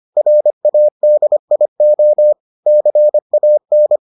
Morse code, heard as a keyed tone, about 25 wpm, 600 Hz. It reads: RADIO CAN